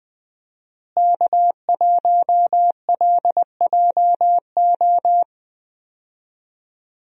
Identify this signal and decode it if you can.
Morse code, heard as a keyed tone, about 20 wpm, 700 Hz. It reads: K1LJO
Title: USA Call Signs - 5 Characters - Repeated 2x - 20wpm